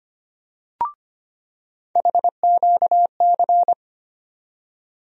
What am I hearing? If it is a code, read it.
Morse code, heard as a keyed tone, about 25 wpm, 700 Hz. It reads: HQC
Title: USA Call Sign Postfixes - 3 Letters 25wpm